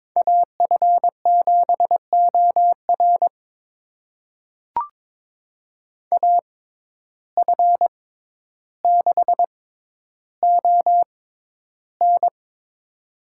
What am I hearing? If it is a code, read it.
Morse code, heard as a keyed tone, about 22 wpm, spaced as for 11 wpm, 700 Hz. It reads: AF7OR AF6ON
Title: USA Call Signs - 5 Characters - 6x Character Spacing - 22wpm